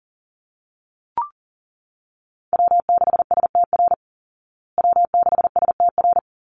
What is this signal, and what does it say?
Morse code, heard as a keyed tone, about 40 wpm, 700 Hz. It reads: W6STR W6STR
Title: Call Signs Repeated 3x - 40wpm